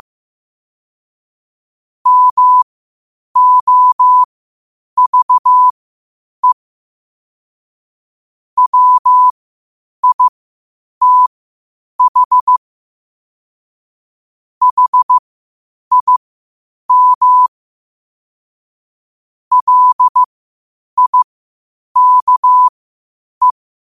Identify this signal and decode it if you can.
Morse code, heard as a keyed tone, about 15 wpm, spaced as for 8 wpm, 1000 Hz. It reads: MOVE WITH HIM LIKE
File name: qotdfiles / 8wpm